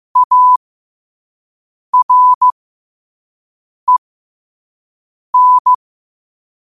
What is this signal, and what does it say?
Morse code, heard as a keyed tone, about 15 wpm, spaced as for 5 wpm, 1000 Hz. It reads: AREN
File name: qotdfiles / 5wpm